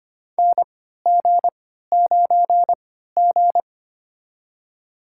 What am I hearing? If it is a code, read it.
Morse code, heard as a keyed tone, about 25 wpm, spaced as for 14 wpm, 700 Hz. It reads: NG9G
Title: USA Call Signs - 4 Characters - 3x Character Spacing - 25wpm